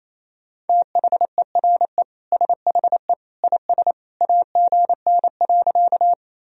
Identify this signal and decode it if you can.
Morse code, heard as a keyed tone, about 28 wpm, 700 Hz. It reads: THERE SHE IS AGN.
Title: Sentences from Top 200 Words and Abbreviations 28wpm